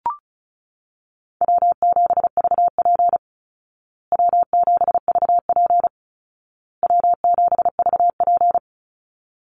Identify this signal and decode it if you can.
Morse code, heard as a keyed tone, about 34 wpm, 700 Hz. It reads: W7VP W7VP W7VP
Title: Call Signs Repeated 2x - 35wpm